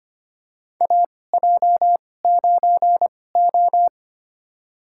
Morse code, heard as a keyed tone, about 25 wpm, spaced as for 18 wpm, 700 Hz. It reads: AJ9O
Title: USA Call Signs - 4 Characters - 2x Character Spacing - 25wpm